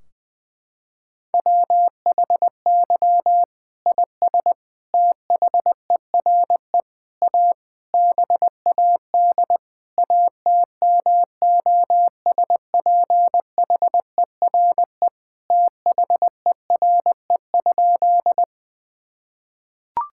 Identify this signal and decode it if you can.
Morse code, heard as a keyed tone, about 21 wpm, 700 Hz. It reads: WHY IS THERE A BAD ATMOSPHERE THERE?
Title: Sentences from Top 1600 Words 20wpm